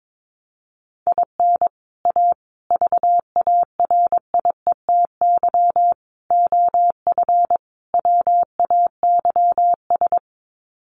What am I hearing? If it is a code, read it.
Morse code, heard as a keyed tone, about 22 wpm, 700 Hz. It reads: IN A VARIETY OF WAYS
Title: Sets of 5 Words Encore 22wpm